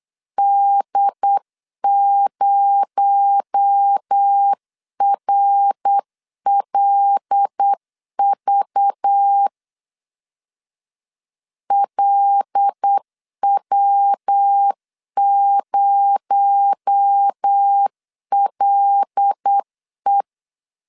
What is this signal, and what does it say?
Morse code, heard as a keyed tone, about 8 wpm, 800 Hz. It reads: D0RLV LW0LE
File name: CW / DET / 1 6 1500C S C 8wpm 5219